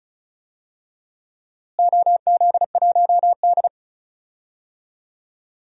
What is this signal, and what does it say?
Morse code, heard as a keyed tone, about 34 wpm, 700 Hz. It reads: OZ1D
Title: DXCC-100 Call Signs - 4 Characters - Repeated 2x - 35wpm